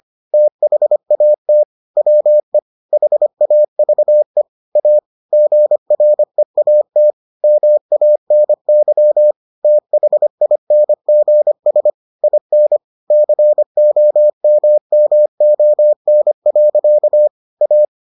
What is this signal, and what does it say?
Morse code, heard as a keyed tone, about 25 wpm, 600 Hz. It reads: THAT WE HAVE A GREAT MANY THINGS IN COMMON. A